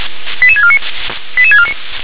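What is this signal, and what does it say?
Five-tone selective call (ZVEI), heard as quick runs of five tones, tones of about 70 ms. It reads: 88649 88649
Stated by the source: ZVEI five-tone selective call received on SDR.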